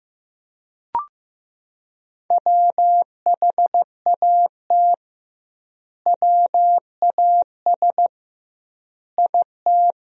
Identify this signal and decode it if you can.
Morse code, heard as a keyed tone, about 16 wpm, 700 Hz. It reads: WHAT WAS IT
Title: 2x Word-Spacing - Sentences from Top 200 Words - 15wpm